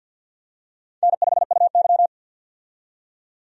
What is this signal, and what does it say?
Morse code, heard as a keyed tone, about 50 wpm, 700 Hz. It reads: N5UY
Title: USA Call Signs - 4 Characters - 50wpm